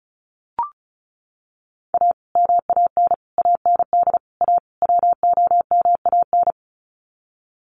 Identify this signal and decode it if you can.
Morse code, heard as a keyed tone, about 34 wpm, 700 Hz. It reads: A MAN AND A WOMAN
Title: Sets of 5 Words 35wpm